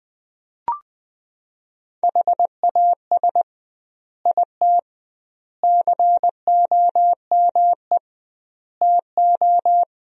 Morse code, heard as a keyed tone, about 20 wpm, 700 Hz. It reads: HAS IT COME TO
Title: 2x Word-Spacing - Sentences from Top 100 Words - 20wpm